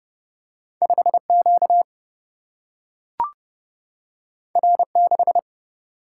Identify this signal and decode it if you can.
Morse code, heard as a keyed tone, about 30 wpm, 700 Hz. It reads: 5Q R6
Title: Single-Letter Single-Number 30wpm